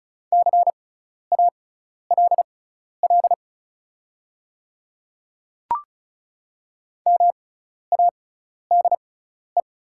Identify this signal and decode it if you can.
Morse code, heard as a keyed tone, about 36 wpm, spaced as for 12 wpm, 700 Hz. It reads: CALL MADE
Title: 6x Character Spacing - Top 200 Words - 35wpm